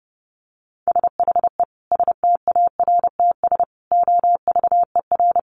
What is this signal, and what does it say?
Morse code, heard as a keyed tone, about 30 wpm, 700 Hz. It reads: SHE STARTS OVER
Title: Easy Phrases 30wpm